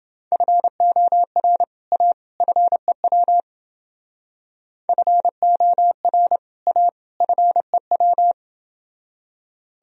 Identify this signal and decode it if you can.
Morse code, heard as a keyed tone, about 28 wpm, 700 Hz. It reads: FOR A FEW FOR A FEW